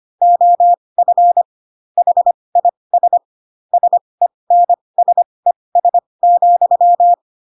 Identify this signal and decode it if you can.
Morse code, heard as a keyed tone, about 25 wpm, spaced as for 20 wpm, 700 Hz. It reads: OF HIS SENSES,
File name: Pinocchio / Pinocchio ch1 15 25 ik1hge